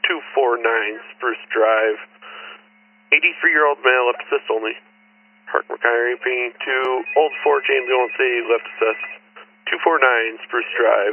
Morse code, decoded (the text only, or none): none